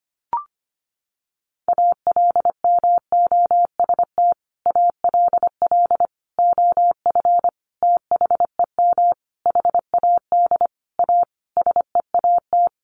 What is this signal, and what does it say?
Morse code, heard as a keyed tone, about 25 wpm, 700 Hz. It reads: ALMOST ALL OF THEM HAD A SEAT
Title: Sentences from Top 1500 Words 25wpm